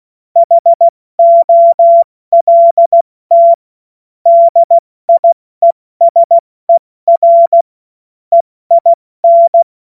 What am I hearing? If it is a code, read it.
Morse code, heard as a keyed tone, about 16 wpm, 680 Hz. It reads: HOLT DIESER EIN
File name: Morsekurs / Witze / Witz 14.November 2023 70 F80